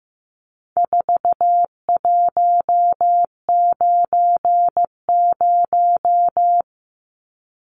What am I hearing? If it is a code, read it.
Morse code, heard as a keyed tone, about 15 wpm, 700 Hz. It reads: 4190
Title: Numbers - 4-Digit 15wpm